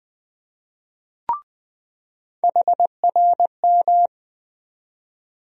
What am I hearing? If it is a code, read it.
Morse code, heard as a keyed tone, about 21 wpm, 700 Hz. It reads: HRM
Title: USA Call Sign Postfixes - 3 Letters 20wpm